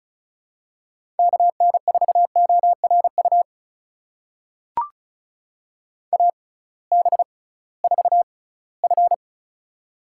Morse code, heard as a keyed tone, about 36 wpm, spaced as for 21 wpm, 700 Hz. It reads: KN4ORU AB4F
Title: USA Call Signs - 6x Character Spacing - 35wpm